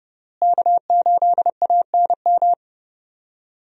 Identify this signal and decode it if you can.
Morse code, heard as a keyed tone, about 30 wpm, 700 Hz. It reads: K8ANM
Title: USA Call Signs - 5 Characters - Repeated 3x - 30wpm